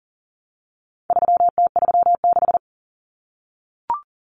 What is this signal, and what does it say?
Morse code, heard as a keyed tone, about 40 wpm, 700 Hz. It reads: 3T36
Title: Common Cut Numbers - 4-Digit 40wpm